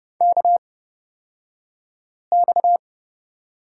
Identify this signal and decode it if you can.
Morse code, heard as a keyed tone, about 30 wpm, spaced as for 5 wpm, 700 Hz. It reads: KX
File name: Lesson 37d - Letter x - Two Characters 30wpm